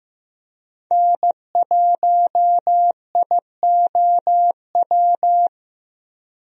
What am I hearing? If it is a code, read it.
Morse code, heard as a keyed tone, about 16 wpm, 700 Hz. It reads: N1IOW
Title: USA Call Signs - 5 Characters - 15wpm